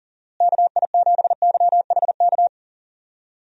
Morse code, heard as a keyed tone, about 40 wpm, 700 Hz. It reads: KI7YHK
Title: USA Call Signs - 6 Characters - Repeated 3x - 40wpm